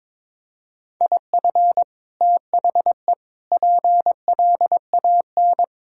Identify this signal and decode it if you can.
Morse code, heard as a keyed tone, about 22 wpm, 700 Hz. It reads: IF THE PLAN
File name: practice / 22wpm / Sentences with 2-Word Phrases - 22wpm